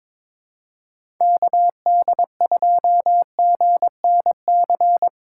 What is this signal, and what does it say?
Morse code, heard as a keyed tone, about 22 wpm, 700 Hz. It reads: KD2GNC
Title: USA Callsign DE Callsign - 6 Characters - 22wpm